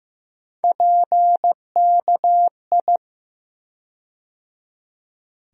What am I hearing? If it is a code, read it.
Morse code, heard as a keyed tone, about 15 wpm, 700 Hz. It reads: PKI
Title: USA Call Sign Postfixes - 3 Letters 15wpm